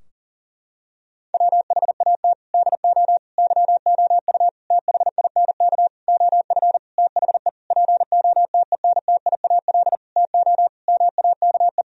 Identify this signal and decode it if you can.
Morse code, heard as a keyed tone, about 40 wpm, 700 Hz. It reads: WHAT DO YOU THINK OF THE POTENTIAL TO MAKE
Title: Sentences from Top 1000 Words - Review Part 4 - 40wpm